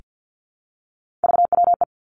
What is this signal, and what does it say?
Morse code, heard as a keyed tone, about 50 wpm, 700 Hz. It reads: VWE
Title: Three Challenging Letter-Number Combinations 50wpm